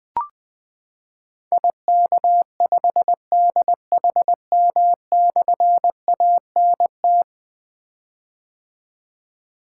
Morse code, heard as a keyed tone, about 20 wpm, 700 Hz. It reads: IK5DHM/ANT
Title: Difficult Call Signs 20wpm